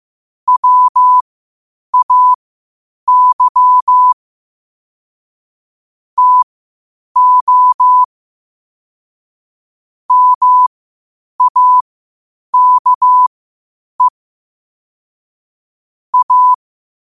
Morse code, heard as a keyed tone, about 15 wpm, spaced as for 8 wpm, 1000 Hz. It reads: WAY TO MAKE A
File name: qotdfiles / 8wpm